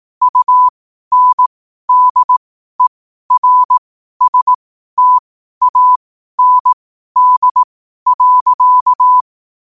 Morse code, heard as a keyed tone, about 18 wpm, spaced as for 12 wpm, 1000 Hz. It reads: UNDERSTAND.
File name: qotdfiles / 12wpm